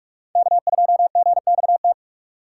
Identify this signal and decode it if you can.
Morse code, heard as a keyed tone, about 46 wpm, 700 Hz. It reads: K2GXT
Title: USA Call Signs - 5 Characters - 4x Character Spacing - 45wpm